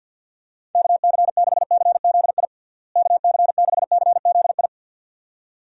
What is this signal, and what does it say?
Morse code, heard as a keyed tone, about 48 wpm, 700 Hz. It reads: KC6CZI KC6CZI
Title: USA Call Signs - 6 Characters - Repeated 2x - 50wpm